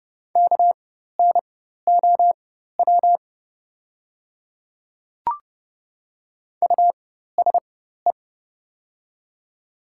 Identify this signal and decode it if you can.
Morse code, heard as a keyed tone, about 30 wpm, spaced as for 14 wpm, 690 Hz. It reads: KNOW USE
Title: 4x Character Spacing - Top 100 Words - 30wpm